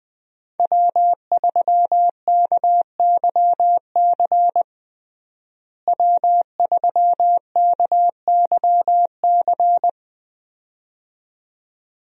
Morse code, heard as a keyed tone, about 20 wpm, 700 Hz. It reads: W3KYC W3KYC
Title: USA Call Signs - 5 Characters - Repeated 2x - 20wpm